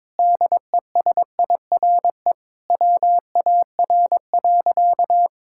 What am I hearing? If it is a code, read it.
Morse code, heard as a keyed tone, about 22 wpm, 700 Hz. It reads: DESIRE WAR.
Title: Sentences from Top 1600 Words 22wpm